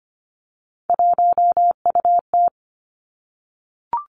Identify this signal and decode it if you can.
Morse code, heard as a keyed tone, about 25 wpm, 700 Hz. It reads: 1UT
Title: Three Challenging Letter-Number Combinations 25wpm